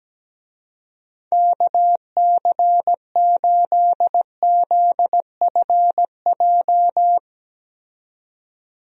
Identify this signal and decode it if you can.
Morse code, heard as a keyed tone, about 17 wpm, 700 Hz. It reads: KC8ZFJ
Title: USA Call Signs - 6 Characters - 17wpm